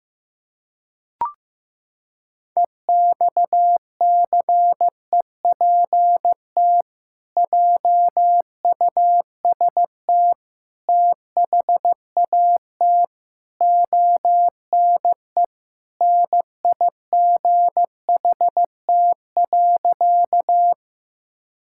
Morse code, heard as a keyed tone, about 15 wpm, 700 Hz. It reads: EXCEPT JUST THAT ONE NIGHT.
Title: Sentences from Top 1600 Words 15wpm